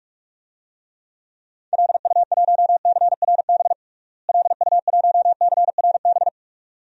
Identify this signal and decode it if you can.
Morse code, heard as a keyed tone, about 42 wpm, 700 Hz. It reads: LU1CRB LU1CRB
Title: DXCC-100 Call Signs - 6 Characters - Repeated 2x - 45wpm